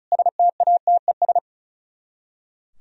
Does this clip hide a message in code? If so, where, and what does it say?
Yes: Morse code from 0.1 to 1.4 s, heard as a keyed tone, about 44 wpm, 700 Hz. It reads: STATES